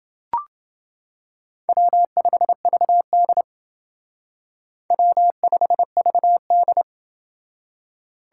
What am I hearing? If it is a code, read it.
Morse code, heard as a keyed tone, about 28 wpm, 700 Hz. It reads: W5VD W5VD